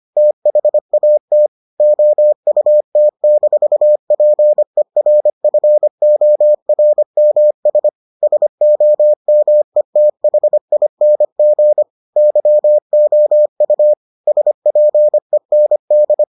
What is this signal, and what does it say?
Morse code, heard as a keyed tone, about 25 wpm, 600 Hz. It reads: THAT OUT-PERFORMS SOMETHING YOU SPEND